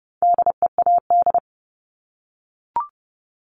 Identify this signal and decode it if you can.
Morse code, heard as a keyed tone, about 30 wpm, 700 Hz. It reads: DEAD
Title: Multi-letter Phonemes 30wpm